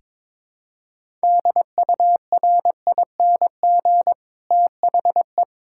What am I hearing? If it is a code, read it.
Morse code, heard as a keyed tone, about 22 wpm, 700 Hz. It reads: DURING THE
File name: Build A Phrase - 2-Word Phrases - 22wpm